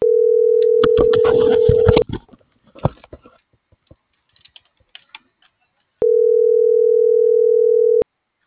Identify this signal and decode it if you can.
Ringback tone (North America), 440 + 480 Hz.